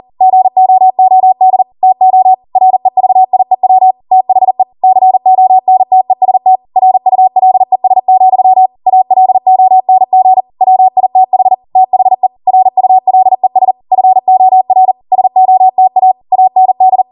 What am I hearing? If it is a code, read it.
Morse code, heard as a keyed tone, about 40 wpm, 750 Hz. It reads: GOOD TO REVIEW THE CONTEST RULES, ALONG WITH THE RULES FOR SOTA AND